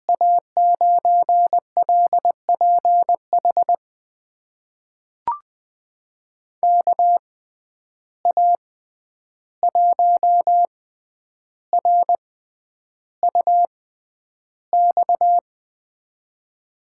Morse code, heard as a keyed tone, about 20 wpm, spaced as for 10 wpm, 700 Hz. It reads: A9LPH KA1RUX
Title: USA Call Signs - 6 Characters - 6x Character Spacing - 20wpm